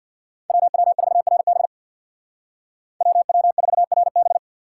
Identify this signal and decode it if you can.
Morse code, heard as a keyed tone, about 59 wpm, 700 Hz. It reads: WW4RB WW4RB
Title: Speed Racing - USA 2x2 Call Signs 50wpm